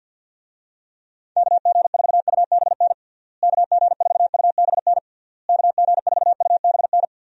Morse code, heard as a keyed tone, about 48 wpm, 700 Hz. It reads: KG4UBN KG4UBN KG4UBN
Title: USA Call Signs - 6 Characters - Repeated 3x - 50wpm